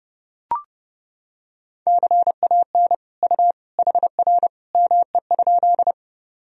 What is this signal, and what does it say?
Morse code, heard as a keyed tone, about 30 wpm, 700 Hz. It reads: CAN U HR ME?
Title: Sentences from Top 300 Words and Abbreviations 30wpm